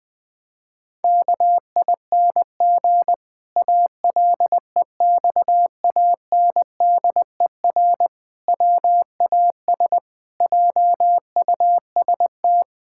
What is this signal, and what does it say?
Morse code, heard as a keyed tone, about 19 wpm, 700 Hz. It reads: KING ALEXANDER WAS JUST
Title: Pangram Challenge v2 - 20wpm